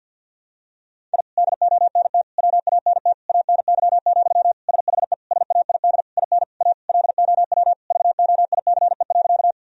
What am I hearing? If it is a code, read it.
Morse code, heard as a keyed tone, about 51 wpm, 700 Hz. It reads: I DONT WANT ANY, SHE SAID IN A LOW VOICE.
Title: Sentences from Top 1000 Words - Review Part 1 - 50wpm